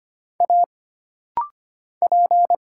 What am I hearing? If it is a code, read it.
Morse code, heard as a keyed tone, about 25 wpm, 700 Hz. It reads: A P
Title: Single Letter-Number - Rapid-Fire 25wpm